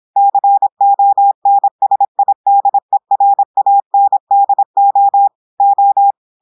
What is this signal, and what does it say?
Morse code, heard as a keyed tone, about 26 wpm, 800 Hz. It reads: CONSIDERANDO O
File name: portugais texte en clair 2 vitesse 26 mots minutes